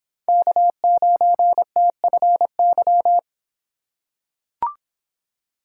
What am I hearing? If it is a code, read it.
Morse code, heard as a keyed tone, about 24 wpm, 700 Hz. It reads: K9TFY